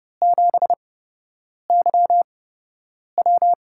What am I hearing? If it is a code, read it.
Morse code, heard as a keyed tone, about 32 wpm, spaced as for 8 wpm, 700 Hz. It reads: 7YW
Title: USA Call Signs - 5 Characters - 8x Character Spacing - 30wpm